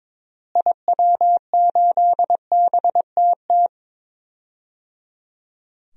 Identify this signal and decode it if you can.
Morse code, heard as a keyed tone, about 21 wpm, 700 Hz. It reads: IW8BTT
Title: DXCC-100 Call Signs - 6 Characters - Repeated 3x - 22wpm